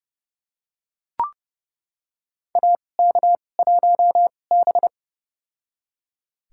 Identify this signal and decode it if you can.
Morse code, heard as a keyed tone, about 31 wpm, spaced as for 22 wpm, 700 Hz. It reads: AK1B
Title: USA Call Signs - 2x Character Spacing - 30wpm